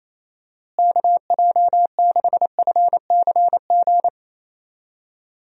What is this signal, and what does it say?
Morse code, heard as a keyed tone, about 28 wpm, 700 Hz. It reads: KJ6FCG
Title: USA Call Signs - 6 Characters - Repeated 3x - 28wpm